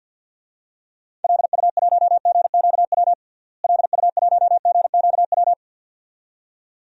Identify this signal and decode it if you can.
Morse code, heard as a keyed tone, about 48 wpm, 700 Hz. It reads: LU1GQW LU1GQW
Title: DXCC-100 Call Signs - 6 Characters - Repeated 2x - 50wpm